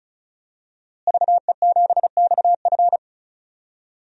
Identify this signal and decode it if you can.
Morse code, heard as a keyed tone, about 36 wpm, 700 Hz. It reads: VE7XF
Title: DXCC-100 Call Signs - 5 Characters - Repeated 2x - 35wpm